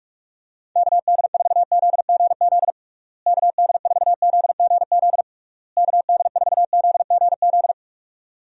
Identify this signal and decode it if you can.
Morse code, heard as a keyed tone, about 46 wpm, 700 Hz. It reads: KD4ZGZ KD4ZGZ KD4ZGZ
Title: USA Call Signs - 6 Characters - Repeated 3x - 45wpm